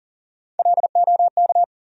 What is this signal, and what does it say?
Morse code, heard as a keyed tone, about 40 wpm, 700 Hz. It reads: LOK
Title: USA Call Sign Postfixes - 3 Letters 40wpm